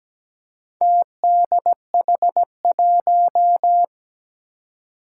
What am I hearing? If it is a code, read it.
Morse code, heard as a keyed tone, about 17 wpm, 700 Hz. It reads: TDH1
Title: Four Challenging Letter-Number Combinations 17wpm